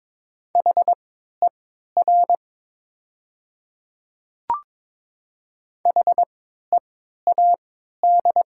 Morse code, heard as a keyed tone, about 22 wpm, spaced as for 12 wpm, 700 Hz. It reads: HER HEAD
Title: 3x Character Spacing - Top 300 Words - 22wpm